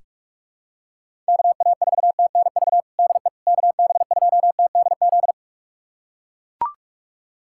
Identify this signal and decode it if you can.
Morse code, heard as a keyed tone, about 41 wpm, 690 Hz. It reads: KA4TNV DE KB2TDZ
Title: USA Callsign DE Callsign - 6 Characters - 45wpm